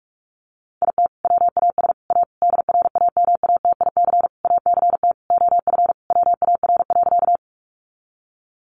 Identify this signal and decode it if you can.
Morse code, heard as a keyed tone, about 46 wpm, 700 Hz. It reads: IT WAS A DRAMATIC ACT OF WAR.
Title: Sentences from Top 1300 Words 45wpm